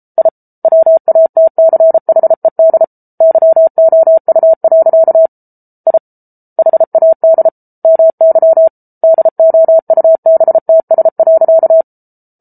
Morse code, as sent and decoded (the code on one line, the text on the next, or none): .. / .-- .- - -.-. .... -.. / -.-- --- ..- .-.-.- / .. / .... .- -.. / -- -.-- / -.. --- ..- -... - ... .-.-.-
I WATCHED YOU. I HAD MY DOUBTS.